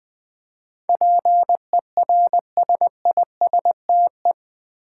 Morse code, heard as a keyed tone, about 20 wpm, 700 Hz. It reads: PERSISTE
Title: Build A Word - 4 Letters At A Time - 10-Letter Words - 20wpm